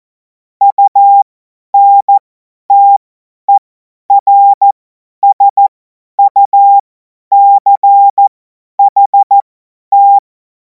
Morse code, heard as a keyed tone, about 14 wpm, spaced as for 10 wpm, 800 Hz. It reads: UNTERSUCHT
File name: Morsekurs / Witze / Witz 14. Juli 2026 50 F70